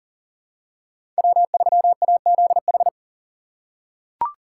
Morse code, heard as a keyed tone, about 40 wpm, 700 Hz. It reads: W3AZH
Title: USA Call Signs - 5 Characters - 4x Character Spacing - 40wpm